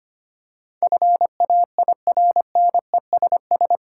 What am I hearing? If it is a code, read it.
Morse code, heard as a keyed tone, about 25 wpm, 690 Hz. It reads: FAIRNESS
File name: 8 Letter Words - First 3 Letters Given - 25wpm